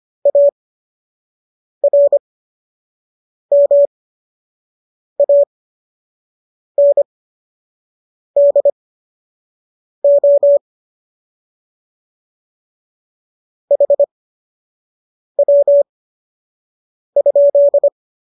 Morse code, heard as a keyed tone, about 25 wpm, spaced as for 6 wpm, 570 Hz. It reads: ARMANDO HW?